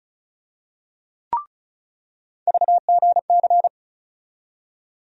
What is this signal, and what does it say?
Morse code, heard as a keyed tone, about 36 wpm, 700 Hz. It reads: VGC
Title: USA Call Sign Postfixes - 3 Letters 35wpm